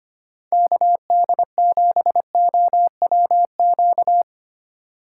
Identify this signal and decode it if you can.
Morse code, heard as a keyed tone, about 28 wpm, 700 Hz. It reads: KD7OWQ
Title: Speed Racing - USA 2x3 Call Signs 25wpm